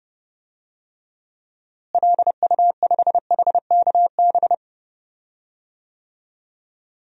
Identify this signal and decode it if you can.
Morse code, heard as a keyed tone, about 30 wpm, 700 Hz. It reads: LU5HKB
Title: DXCC-100 Call Signs - 6 Characters - Repeated 3x - 30wpm